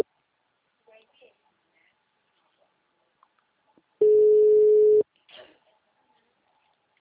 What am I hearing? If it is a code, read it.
Ringback tone (Europe, 425 Hz), with speech.